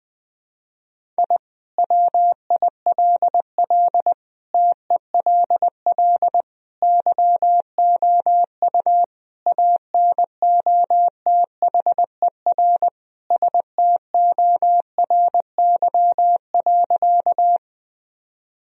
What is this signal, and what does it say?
Morse code, heard as a keyed tone, about 20 wpm, 700 Hz. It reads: I WILL TELL YOU ANOTHER STORY.
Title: Sentences from Top 1000 Words - Review Part 1 - 20wpm